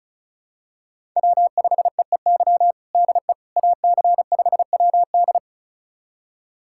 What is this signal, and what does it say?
Morse code, heard as a keyed tone, about 36 wpm, 700 Hz. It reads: W5EEY DE AC5WD
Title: USA Callsign DE Callsign - 5 Characters - 35wpm